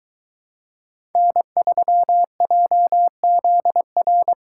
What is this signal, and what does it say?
Morse code, heard as a keyed tone, about 22 wpm, 700 Hz. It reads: N3JZR